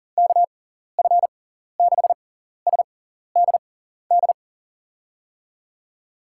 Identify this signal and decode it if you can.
Morse code, heard as a keyed tone, about 39 wpm, spaced as for 14 wpm, 700 Hz. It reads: KF6SDD
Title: USA Call Signs - 6 Characters - 6x Character Spacing - 40wpm